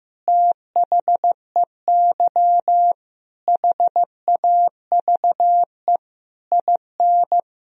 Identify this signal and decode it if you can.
Morse code, heard as a keyed tone, about 15 wpm, 700 Hz. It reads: THEY HAVE IN
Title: Sentences from Top 1000 Words - Review Part 2 - 15wpm